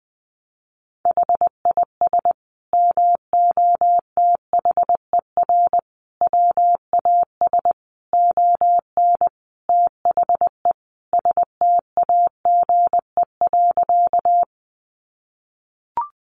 Morse code, heard as a keyed tone, about 19 wpm, 700 Hz. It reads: HIS MOTHER WAS ON THE STAGE.
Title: Sentences from Top 1200 Words 20wpm